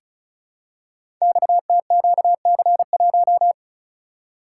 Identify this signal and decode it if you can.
Morse code, heard as a keyed tone, about 36 wpm, 690 Hz. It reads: XTQC1
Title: Code Groups - 5 Letter:Numbers - 35wpm